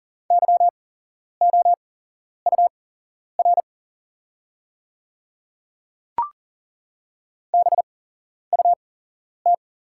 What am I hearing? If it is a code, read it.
Morse code, heard as a keyed tone, about 40 wpm, spaced as for 11 wpm, 690 Hz. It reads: YOUR BUT
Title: 8x Character Spacing - Top 100 Words - 40wpm